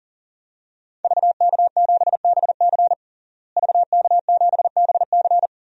Morse code, heard as a keyed tone, about 40 wpm, 700 Hz. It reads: VK7BC VK7BC
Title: DXCC-100 Call Signs - 5 Characters - Repeated 3x - 40wpm